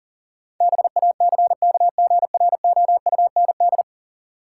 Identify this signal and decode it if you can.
Morse code, heard as a keyed tone, about 40 wpm, 700 Hz. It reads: BACKGROUND
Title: Sentences from Top 1300 Words - New Word Drill Alt - 40wpm